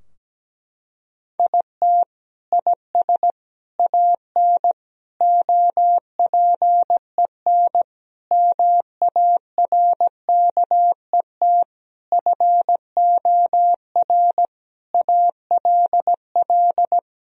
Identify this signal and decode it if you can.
Morse code, heard as a keyed tone, about 17 wpm, 700 Hz. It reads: IT IS AN OPEN MARKET FOR ALL
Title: Sentences from Top 1000 Words - Review Part 3 - 17wpm